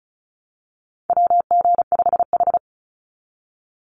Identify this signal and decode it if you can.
Morse code, heard as a keyed tone, about 34 wpm, 700 Hz. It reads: WG5H
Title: Call Signs Repeated 3x - 35wpm